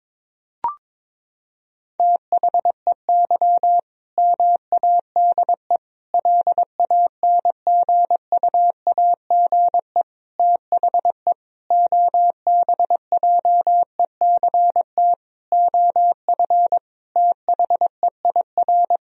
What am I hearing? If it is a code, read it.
Morse code, heard as a keyed tone, about 22 wpm, 700 Hz. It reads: THEY MADE LANGUAGE THE OBJECT OF THEIR
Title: Sentences from Top 1000 Words - Review Part 3 - 22wpm